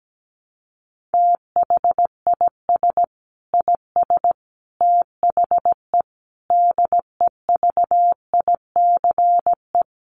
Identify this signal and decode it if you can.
Morse code, heard as a keyed tone, about 17 wpm, 700 Hz. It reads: THIS IS THE DEVICE
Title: Sentences from Top 1500 Words 17wpm